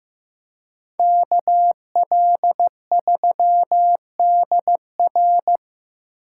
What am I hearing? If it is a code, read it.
Morse code, heard as a keyed tone, about 17 wpm, 700 Hz. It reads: KL3DR
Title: Speed Racing - USA 2x2 Call Signs 15wpm